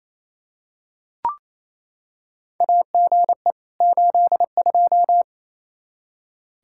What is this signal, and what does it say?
Morse code, heard as a keyed tone, about 28 wpm, 700 Hz. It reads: AGE 82
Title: Practice QSO - Exchange 3 - More Info - 28wpm